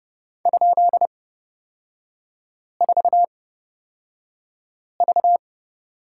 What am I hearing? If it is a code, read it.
Morse code, heard as a keyed tone, about 30 wpm, spaced as for 5 wpm, 700 Hz. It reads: ?4V
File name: Lesson 36e - BT - Three Characters 30wpm